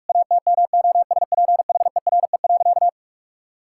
Morse code, heard as a keyed tone, about 46 wpm, 700 Hz. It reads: ATMOSPHERE.
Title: Sentences from Top 1600 Words 45wpm